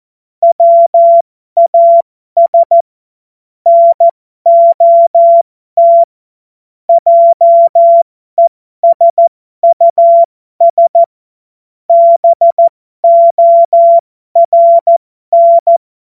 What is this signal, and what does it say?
Morse code, heard as a keyed tone, about 14 wpm, spaced as for 12 wpm, 680 Hz. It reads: WAS NOT JESUS BORN
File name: Morsekurs / Witze